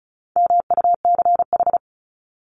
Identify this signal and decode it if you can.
Morse code, heard as a keyed tone, about 34 wpm, 700 Hz. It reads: MUCH
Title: Top 300 Words 35wpm